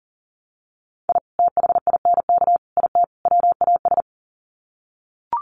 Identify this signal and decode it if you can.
Morse code, heard as a keyed tone, about 40 wpm, 700 Hz. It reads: I THINK IT WAS
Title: Sets of 4 Words 40wpm